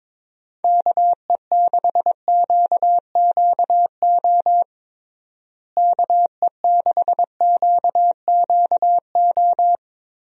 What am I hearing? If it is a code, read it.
Morse code, heard as a keyed tone, about 22 wpm, 700 Hz. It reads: KE6QQO KE6QQO
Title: USA Call Signs - 6 Characters - Repeated 3x - 22wpm